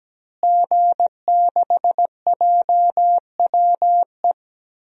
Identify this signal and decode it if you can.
Morse code, heard as a keyed tone, about 17 wpm, 700 Hz. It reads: G6JWE
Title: DXCC-100 Call Signs - 5 Characters - Repeated 3x - 17wpm